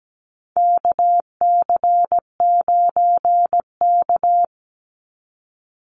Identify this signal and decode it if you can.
Morse code, heard as a keyed tone, about 17 wpm, 700 Hz. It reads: KC9K
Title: Call Signs - 17wpm